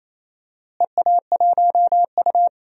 Morse code, heard as a keyed tone, about 28 wpm, 700 Hz. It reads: EA1U
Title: DXCC-100 Call Signs - 4 Characters - Repeated 2x - 28wpm